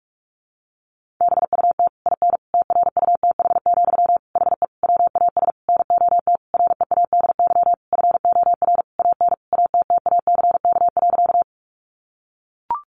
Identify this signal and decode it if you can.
Morse code, heard as a keyed tone, about 44 wpm, 700 Hz. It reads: BUT IN TRUTH, HE WAS NOT READY FOR AN ATTACK.
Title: Sentences from Top 1500 Words 45wpm